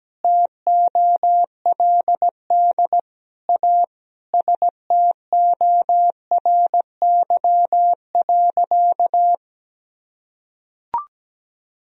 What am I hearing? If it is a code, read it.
Morse code, heard as a keyed tone, about 17 wpm, 700 Hz. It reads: TOLD A STORY.
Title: Sentences from Top 1000 Words - Review Part 1 - 17wpm